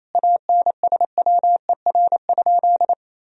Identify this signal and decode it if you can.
Morse code, heard as a keyed tone, about 28 wpm, 700 Hz. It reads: ANSWER?
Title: Sentences from Top 1000 Words - Review Part 2 - 28wpm